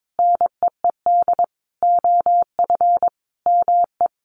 Morse code, heard as a keyed tone, about 22 wpm, 700 Hz. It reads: NEED OF ME
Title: Sentences from Top 1400 Words 22wpm